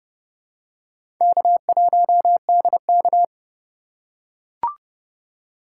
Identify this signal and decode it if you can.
Morse code, heard as a keyed tone, about 31 wpm, 700 Hz. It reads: K1DK
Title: USA Call Signs - 4 Characters - 2x Character Spacing - 30wpm